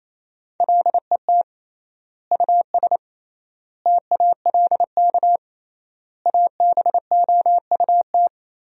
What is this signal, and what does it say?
Morse code, heard as a keyed tone, about 28 wpm, 700 Hz. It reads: LET US TALK ABOUT
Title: 3x Word-Spacing - Sentences from Top 400 Words - 28wpm